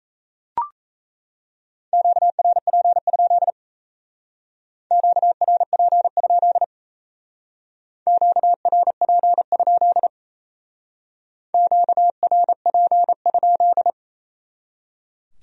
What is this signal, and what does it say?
Morse code, heard as a keyed tone, about 33 wpm, 700 Hz. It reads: QRP? QRP? QRP? QRP?
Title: Speed-Racing Q-Codes - 28wpm